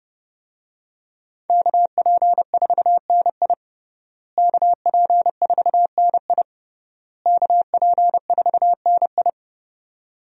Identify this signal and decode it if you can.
Morse code, heard as a keyed tone, about 30 wpm, 700 Hz. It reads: KP4NI KP4NI KP4NI
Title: USA Call Signs - 5 Characters - Repeated 3x - 30wpm